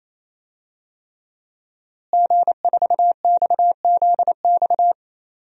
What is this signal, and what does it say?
Morse code, heard as a keyed tone, about 27 wpm, 700 Hz. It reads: G4XZX
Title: DXCC-100 Call Signs - 5 Characters - Repeated 3x - 28wpm